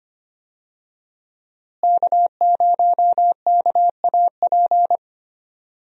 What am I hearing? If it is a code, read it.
Morse code, heard as a keyed tone, about 25 wpm, 700 Hz. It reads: K0KAP
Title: USA Call Signs - 5 Characters - Repeated 3x - 25wpm